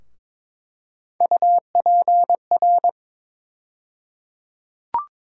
Morse code, heard as a keyed tone, about 22 wpm, 700 Hz. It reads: UPR
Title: USA Call Sign Postfixes - 3 Letters 22wpm